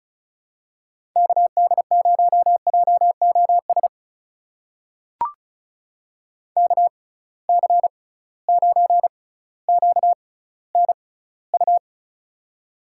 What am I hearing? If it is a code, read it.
Morse code, heard as a keyed tone, about 36 wpm, spaced as for 18 wpm, 700 Hz. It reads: KD0JOS KC9QNU
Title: USA Call Signs - 6 Characters - 6x Character Spacing - 35wpm